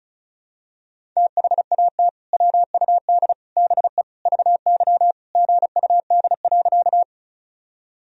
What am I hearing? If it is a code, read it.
Morse code, heard as a keyed tone, about 36 wpm, 700 Hz. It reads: THAT WUD BE VY GUD.
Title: Sentences from Top 200 Words and Abbreviations 35wpm